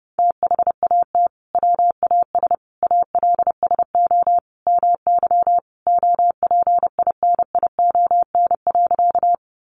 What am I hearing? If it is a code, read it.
Morse code, heard as a keyed tone, about 30 wpm, 700 Hz. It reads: THAT WAS ALSO MY OPINION.
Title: Sentences from Top 1200 Words 30wpm